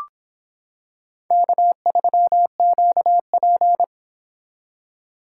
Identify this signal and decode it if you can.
Morse code, heard as a keyed tone, about 26 wpm, 700 Hz. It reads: K3QP